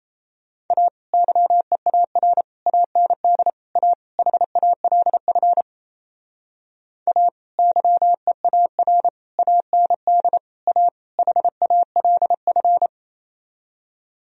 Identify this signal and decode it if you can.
Morse code, heard as a keyed tone, about 36 wpm, 700 Hz. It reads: A YEAR AND A HALF A YEAR AND A HALF